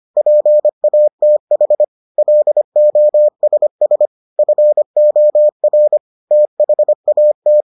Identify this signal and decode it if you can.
Morse code, heard as a keyed tone, about 25 wpm, 600 Hz. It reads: PATH LOSS FOR THAT